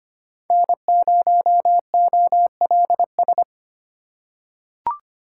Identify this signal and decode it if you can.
Morse code, heard as a keyed tone, about 25 wpm, 700 Hz. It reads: N0OLS
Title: USA Call Signs - 4x Character Spacing - 25wpm